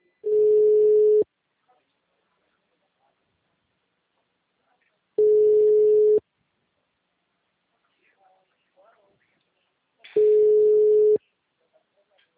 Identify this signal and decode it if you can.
Ringback tone (Europe, 425 Hz).
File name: out- -2010